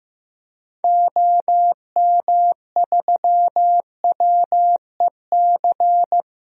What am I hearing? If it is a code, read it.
Morse code, heard as a keyed tone, about 16 wpm, 700 Hz. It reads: OM3WEC
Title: DXCC-100 Call Signs - 6 Characters - Repeated 2x - 15wpm